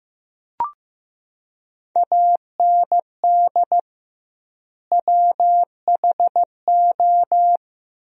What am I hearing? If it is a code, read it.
Morse code, heard as a keyed tone, about 16 wpm, 700 Hz. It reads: AND WHO
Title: 2x Word-Spacing - Sentences from Top 100 Words - 15wpm